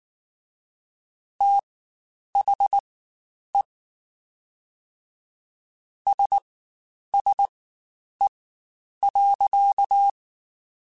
Morse code, heard as a keyed tone, about 19 wpm, spaced as for 9 wpm, 800 Hz. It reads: THE SSE.
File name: Spaced Characters / Spaced Characters-mp3s / Weather 4 14 sp @18